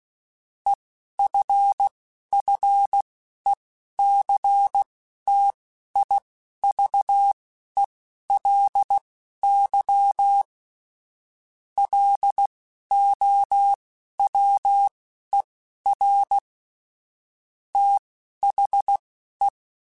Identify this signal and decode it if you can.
Morse code, heard as a keyed tone, about 16 wpm, spaced as for 11 wpm, 800 Hz. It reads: EFFECTIVELY LOWER THE
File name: Technical Subjects / Technical Subjects-mp3s / XC @10